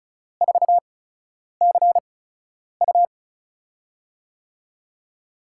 Morse code, heard as a keyed tone, about 36 wpm, spaced as for 10 wpm, 700 Hz. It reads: 4CU
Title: USA Call Signs - 8x Character Spacing - 35wpm